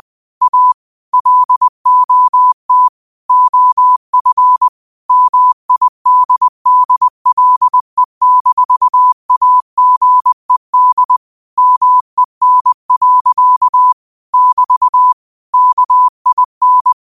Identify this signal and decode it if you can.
Morse code, heard as a keyed tone, about 20 wpm, 1000 Hz. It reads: A LOT OF MIDDLE-AGED MEN. <BT> KIN
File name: qotdfiles / 20wpm